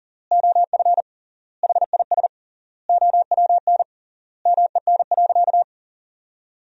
Morse code, heard as a keyed tone, about 40 wpm, 700 Hz. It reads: OF HIS OWN MEN.
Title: 3x Word-Spacing - Sentences from Top 300 Words - 40wpm